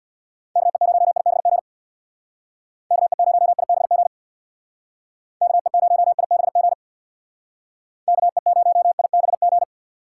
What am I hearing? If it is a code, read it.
Morse code, heard as a keyed tone, about 65 wpm, 700 Hz. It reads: KE0IBG KE0IBG KE0IBG KE0IBG